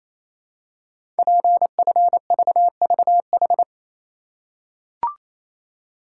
Morse code, heard as a keyed tone, about 29 wpm, 690 Hz. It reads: PFVVH
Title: Code Groups - 5 Letters - 28wpm